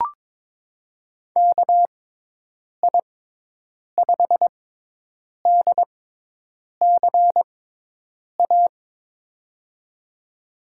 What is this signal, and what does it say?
Morse code, heard as a keyed tone, about 22 wpm, spaced as for 8 wpm, 700 Hz. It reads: KI5DCA
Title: USA Call Signs - 6x Character Spacing - 22wpm